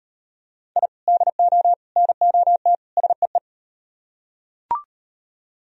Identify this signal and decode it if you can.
Morse code, heard as a keyed tone, about 38 wpm, 700 Hz. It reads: I DO NOT SEE